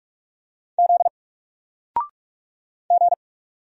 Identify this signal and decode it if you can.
Morse code, heard as a keyed tone, about 44 wpm, 700 Hz. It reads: Z G
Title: Single Letter-Number Alt 45wpm